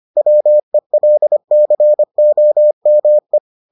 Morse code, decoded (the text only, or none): WELCOME